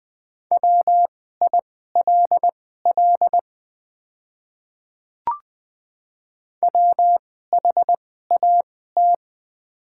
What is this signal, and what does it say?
Morse code, heard as a keyed tone, about 20 wpm, spaced as for 14 wpm, 700 Hz. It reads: WILL WHAT
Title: 2x Character Spacing - Top 100 Words - 20wpm